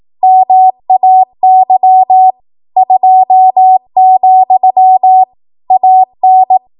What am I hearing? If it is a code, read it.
Morse code, heard as a keyed tone, about 18 wpm, 750 Hz. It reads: MAY 2, AN
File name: Morse / 18WPM